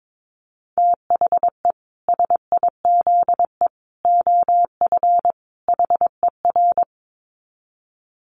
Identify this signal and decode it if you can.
Morse code, heard as a keyed tone, about 22 wpm, 700 Hz. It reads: THE SIZE OF HER
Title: Dit Challenge 22wpm